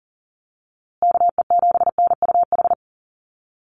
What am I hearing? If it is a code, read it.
Morse code, heard as a keyed tone, about 40 wpm, 700 Hz. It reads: KE7NUH
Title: USA Call Signs Repeated 3x 40wpm